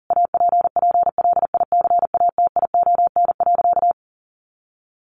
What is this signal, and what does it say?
Morse code, heard as a keyed tone, about 40 wpm, 700 Hz. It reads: APPLICATION.
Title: Sentences from Top 1200 Words 40wpm